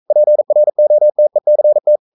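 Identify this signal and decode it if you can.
Morse code, heard as a keyed tone, about 42 wpm, 600 Hz. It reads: PROTECT
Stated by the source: Morse code sending of an English novel at 42 wpm.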